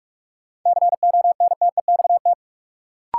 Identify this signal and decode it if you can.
Morse code, heard as a keyed tone, about 46 wpm, 700 Hz. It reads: CONTEXT
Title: Sets of 4 Words - New Word Drill - 45wpm